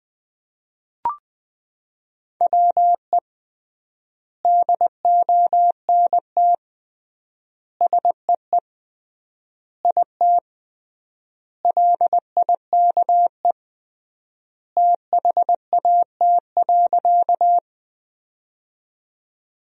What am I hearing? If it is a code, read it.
Morse code, heard as a keyed tone, about 20 wpm, 700 Hz. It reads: WE DONT SEE IT LIKE THAT.
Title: 3x Word-Spacing - Sentences from Top 300 Words - 20wpm